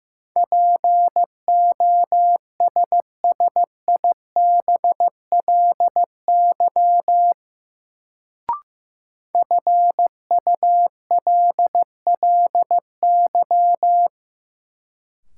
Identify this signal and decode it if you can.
Morse code, heard as a keyed tone, about 15 wpm, 700 Hz. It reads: POSSIBLY FULLY
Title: Sentences from Top 1300 Words - New Word Drill - 15wpm